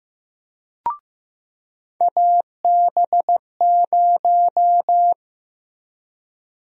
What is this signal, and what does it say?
Morse code, heard as a keyed tone, about 15 wpm, 700 Hz. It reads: AB0
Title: USA Call Sign Prefixes - Repeated 2x - 15wpm